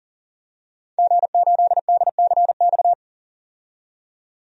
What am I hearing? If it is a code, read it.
Morse code, heard as a keyed tone, about 40 wpm, 700 Hz. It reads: G8DCX
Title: DXCC-100 Call Signs - 5 Characters - 40wpm